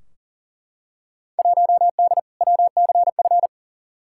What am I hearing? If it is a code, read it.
Morse code, heard as a keyed tone, about 40 wpm, 700 Hz. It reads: 1D WCF
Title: Field Day Exchanges - 40wpm